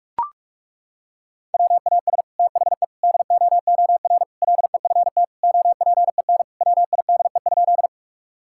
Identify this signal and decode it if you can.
Morse code, heard as a keyed tone, about 41 wpm, 700 Hz. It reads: WAS THE DOOR LEFT OPEN WIDE?
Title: Sentences from Top 1000 Words - Review Part 6 - 45wpm